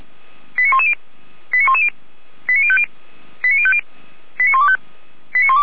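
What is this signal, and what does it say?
Five-tone selective call (ZVEI), heard as quick runs of five tones, tones of about 70 ms. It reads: 89110 89110 89050 89050 89125
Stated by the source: ZVEI five-tone selective call received on an SDR.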